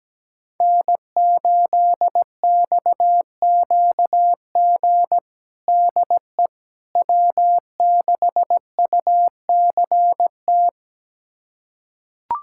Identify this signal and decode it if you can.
Morse code, heard as a keyed tone, about 17 wpm, 700 Hz. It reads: N8XQG DE W6UCT
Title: USA Callsign DE Callsign - 5 Characters - 17wpm